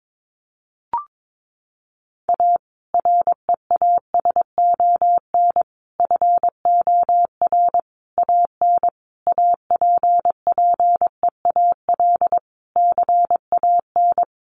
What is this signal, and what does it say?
Morse code, heard as a keyed tone, about 22 wpm, 700 Hz. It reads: A REASON FOR AN APPEAL CAN
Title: Sentences from Top 1300 Words 22wpm